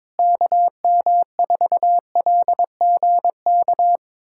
Morse code, heard as a keyed tone, about 22 wpm, 700 Hz. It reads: KM4LGK
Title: USA Callsign DE Callsign - 6 Characters - 22wpm